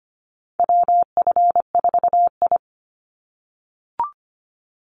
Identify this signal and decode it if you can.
Morse code, heard as a keyed tone, about 25 wpm, 700 Hz. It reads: WF4I
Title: NA SOTA Chaser Callsigns 25wpm